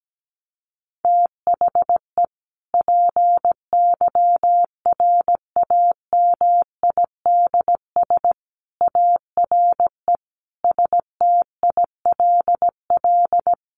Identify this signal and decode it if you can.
Morse code, heard as a keyed tone, about 17 wpm, 700 Hz. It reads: THE PYRAMIDS ARE STILL
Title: A Sound Of Thunder 17wpm